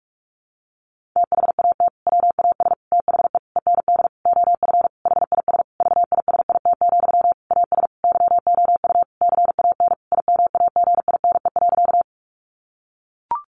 Morse code, heard as a keyed tone, about 44 wpm, 700 Hz. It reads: THAT WAS THE END OF HIS VISIT, AS YOU CAN IMAGINE.
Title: Sentences from Top 1400 Words 45wpm